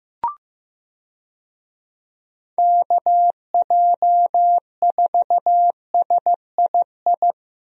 Morse code, heard as a keyed tone, about 15 wpm, 700 Hz. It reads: KJ4SII
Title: USA Call Signs - 6 Characters - Repeated 2x - 15wpm